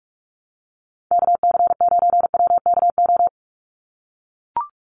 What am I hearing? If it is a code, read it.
Morse code, heard as a keyed tone, about 44 wpm, 700 Hz. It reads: KC9WKO
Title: USA Call Signs 45wpm